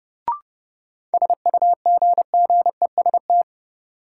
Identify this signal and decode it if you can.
Morse code, heard as a keyed tone, about 30 wpm, 700 Hz. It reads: SUGGEST
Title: Sentences from Top 1300 Words - New Word Drill - 30wpm